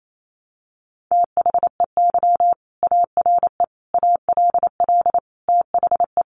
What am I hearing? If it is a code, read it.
Morse code, heard as a keyed tone, about 28 wpm, 700 Hz. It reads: THEY ARE ALL THE
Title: Sentences from Top 600 Words 28wpm